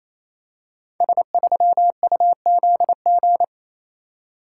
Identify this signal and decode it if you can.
Morse code, heard as a keyed tone, about 27 wpm, 700 Hz. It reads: S3UZG
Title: Code Groups - 5 Letter:Numbers - 28wpm